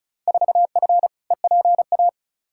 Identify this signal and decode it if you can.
Morse code, heard as a keyed tone, about 36 wpm, 700 Hz. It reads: 4F EPA
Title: Field Day Exchanges - 35wpm